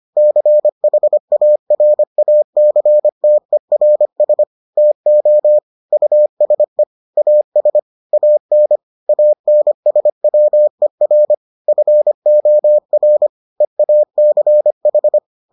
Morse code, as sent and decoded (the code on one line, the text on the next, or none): -.-. .... .- .-. .- -.-. - . .-. ... / - --- / ..- ... / .- ... / .- -. / .- -. ... .-- . .-. / ..-. --- .-. / . .- -.-. ....
CHARACTERS TO USE AS AN ANSWER FOR EACH